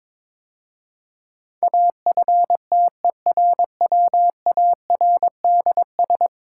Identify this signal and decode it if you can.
Morse code, heard as a keyed tone, about 22 wpm, 700 Hz. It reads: AFTERWARDS
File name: practice / 22wpm / 10 Letter Words - First 3 Letters Given - 22wpm